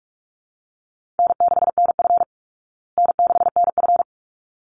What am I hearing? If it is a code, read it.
Morse code, heard as a keyed tone, about 44 wpm, 700 Hz. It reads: N6NF N6NF
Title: Easy Call Signs Repeated 2x - 45wpm